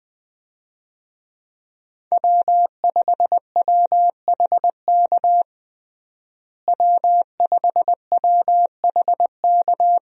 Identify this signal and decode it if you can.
Morse code, heard as a keyed tone, about 21 wpm, 700 Hz. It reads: W5WHK W5WHK
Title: USA Call Signs - 5 Characters - Repeated 3x - 20wpm